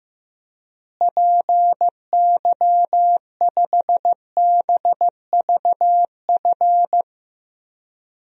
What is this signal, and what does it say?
Morse code, heard as a keyed tone, about 15 wpm, 700 Hz. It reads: PY5BVF
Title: DXCC-100 Call Signs - 6 Characters - Repeated 3x - 15wpm